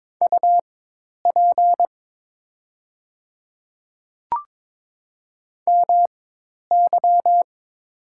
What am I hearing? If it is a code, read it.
Morse code, heard as a keyed tone, about 22 wpm, spaced as for 10 wpm, 700 Hz. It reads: UP MY
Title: 4x Character Spacing - Top 300 Words - 22wpm